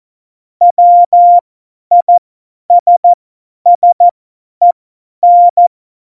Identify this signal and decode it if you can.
Morse code, heard as a keyed tone, about 14 wpm, spaced as for 10 wpm, 700 Hz. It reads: WISSEN
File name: Morsekurs / Witze / Witz 09. Oktober 2018, 50 F70